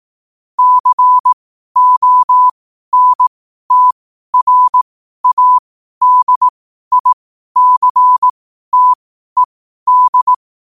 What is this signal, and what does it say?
Morse code, heard as a keyed tone, about 18 wpm, spaced as for 12 wpm, 1000 Hz. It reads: CONTRADICTED
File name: qotdfiles / 12wpm